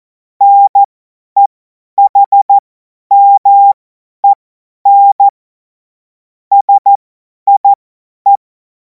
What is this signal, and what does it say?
Morse code, heard as a keyed tone, about 14 wpm, spaced as for 10 wpm, 800 Hz. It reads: NEHMEN SIE
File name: Morsekurs / Witze / Witz 03. März 2026 50 F70